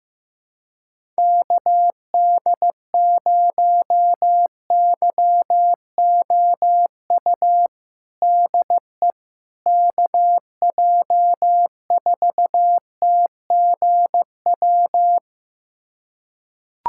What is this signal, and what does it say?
Morse code, heard as a keyed tone, about 15 wpm, 700 Hz. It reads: KD0YOU DE KJ4TGW
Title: USA Callsign DE Callsign - 6 Characters - 15wpm